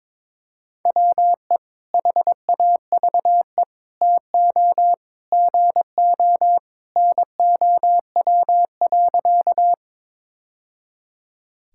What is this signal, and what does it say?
Morse code, heard as a keyed tone, about 22 wpm, 700 Hz. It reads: WE HAVE TO GO NOW.
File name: practice / 22wpm / Sentences with 3-Word Phrases - 22wpm